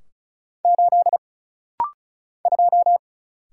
Morse code, heard as a keyed tone, about 35 wpm, 700 Hz. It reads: 8 2
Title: Single Letter-Number - Rapid-Fire 35wpm